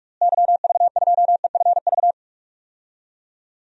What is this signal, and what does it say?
Morse code, heard as a keyed tone, about 46 wpm, 700 Hz. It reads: YV2EFV
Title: DXCC-100 Call Signs - 6 Characters - Repeated 2x - 45wpm